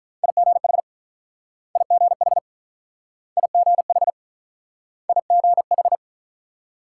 Morse code, heard as a keyed tone, about 46 wpm, 700 Hz. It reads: IGH IGH IGH IGH